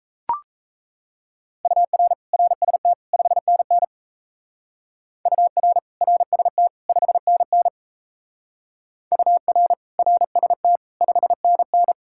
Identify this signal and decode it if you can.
Morse code, heard as a keyed tone, about 35 wpm, 700 Hz. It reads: UR RST 5NN UR RST 5NN UR RST 5NN
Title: Speed-Racing Common QSO Phrases - 28wpm